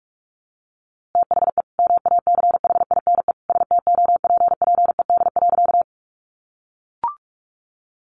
Morse code, heard as a keyed tone, about 44 wpm, 700 Hz. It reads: THE MACHINE STOPPED.
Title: A Sound Of Thunder 45wpm